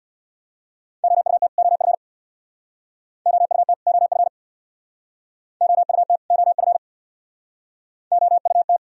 Morse code, heard as a keyed tone, about 66 wpm, 700 Hz. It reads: OUT OF OUT OF OUT OF OUT